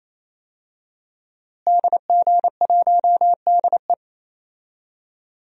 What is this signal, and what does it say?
Morse code, heard as a keyed tone, about 28 wpm, 700 Hz. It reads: DG1DE
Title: DXCC-100 Call Signs - 5 Characters - Repeated 3x - 28wpm